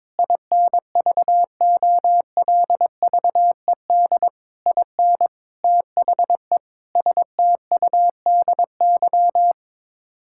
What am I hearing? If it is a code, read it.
Morse code, heard as a keyed tone, about 23 wpm, 700 Hz. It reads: INVOLVED IN THE STUDY
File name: practice / 22wpm / Sets of 7 Words - 22wpm